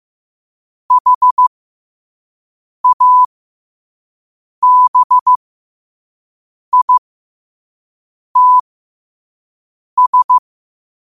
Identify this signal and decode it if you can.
Morse code, heard as a keyed tone, about 15 wpm, spaced as for 5 wpm, 1000 Hz. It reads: HABITS
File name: qotdfiles / 5wpm